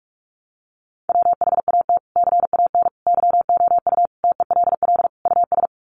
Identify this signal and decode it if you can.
Morse code, heard as a keyed tone, about 44 wpm, 700 Hz. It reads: WHAT CAN YOU TELL US
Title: Sets of 5 Words Encore 45wpm